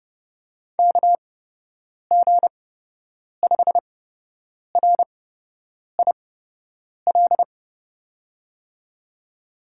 Morse code, heard as a keyed tone, about 30 wpm, spaced as for 8 wpm, 700 Hz. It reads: KG5RIL
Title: USA Call Signs - 6 Characters - 8x Character Spacing - 30wpm